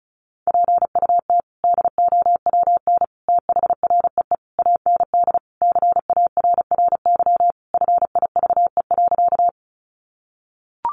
Morse code, heard as a keyed tone, about 34 wpm, 700 Hz. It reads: PUT DOWN THREE AND CARRY FIVE.
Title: Sentences from Top 400 Words 35wpm